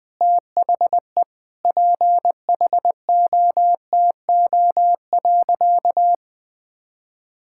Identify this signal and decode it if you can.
Morse code, heard as a keyed tone, about 20 wpm, 700 Hz. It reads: THE PHOTO.
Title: Sentences with 4-Word Phrases - 20wpm